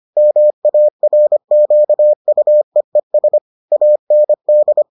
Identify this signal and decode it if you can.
Morse code, heard as a keyed tone, about 25 wpm, 600 Hz. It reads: MARQUEES AND